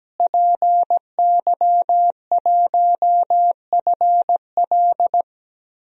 Morse code, heard as a keyed tone, about 17 wpm, 700 Hz. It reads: PY1FL
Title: DXCC-100 Call Signs - 5 Characters - Repeated 2x - 17wpm